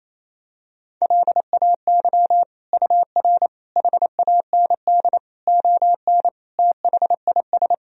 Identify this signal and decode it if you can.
Morse code, heard as a keyed tone, about 28 wpm, 700 Hz. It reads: LAY UR HAND ON THIS
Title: Sentences from Top 400 Words and Abbreviations 28wpm